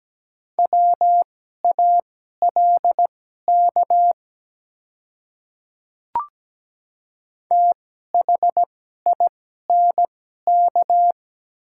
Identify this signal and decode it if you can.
Morse code, heard as a keyed tone, about 17 wpm, spaced as for 12 wpm, 700 Hz. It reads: WALK THINK
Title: 2x Character Spacing - Top 300 Words - 17wpm